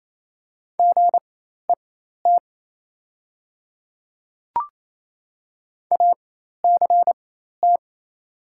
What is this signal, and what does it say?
Morse code, heard as a keyed tone, about 28 wpm, spaced as for 13 wpm, 700 Hz. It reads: GET ACT
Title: 4x Character Spacing - Top 500 Words - 28wpm